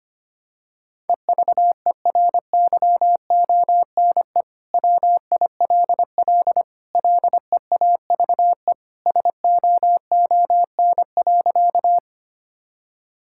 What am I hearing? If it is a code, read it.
Morse code, heard as a keyed tone, about 25 wpm, 700 Hz. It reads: EVERYONE WILL LEAVE SOON.
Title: Sentences from Top 1000 Words - Review Part 5 - 25wpm